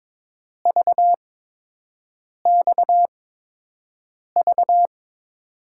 Morse code, heard as a keyed tone, about 22 wpm, spaced as for 6 wpm, 700 Hz. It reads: VXV
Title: USA Call Signs - 8x Character Spacing - 22wpm